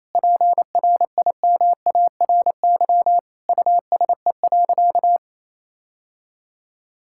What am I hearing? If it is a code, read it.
Morse code, heard as a keyed tone, about 27 wpm, 700 Hz. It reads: PRIMARY USE.
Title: Sentences from Top 1000 Words - Review Part 5 - 28wpm